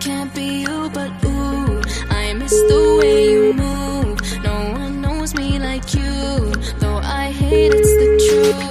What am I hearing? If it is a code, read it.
Ringback tone (Europe, 425 Hz), with noise.